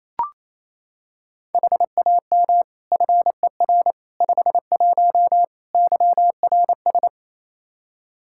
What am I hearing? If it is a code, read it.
Morse code, heard as a keyed tone, about 28 wpm, 700 Hz. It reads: HAM FER 51 YRS
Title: Practice QSO - Exchange 3 - More Info - 28wpm